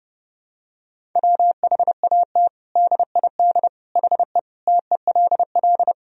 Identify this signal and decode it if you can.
Morse code, heard as a keyed tone, about 30 wpm, 700 Hz. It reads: WHAT DID HE TELL
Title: Sentences from Top 200 Words and Abbreviations 30wpm